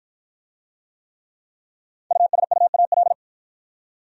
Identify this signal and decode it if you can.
Morse code, heard as a keyed tone, about 53 wpm, 700 Hz. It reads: USUAL